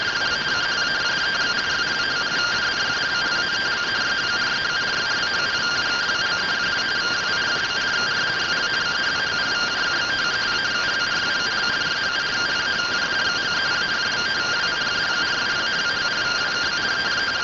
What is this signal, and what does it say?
Two-tone teleprinter signal, about 170 Hz shift.